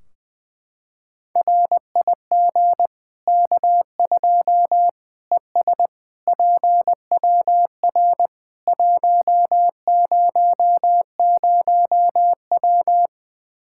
Morse code, heard as a keyed tone, about 20 wpm, 700 Hz. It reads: RIG K2 ES PWR 100W
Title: Practice QSO - Exchange 2 - Info - 20wpm